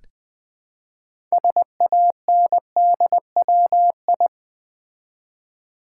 Morse code, heard as a keyed tone, about 20 wpm, 700 Hz. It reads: SANDWI
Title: Build A Word - 3 Letters At A Time - 8-Letter Words - 20wpm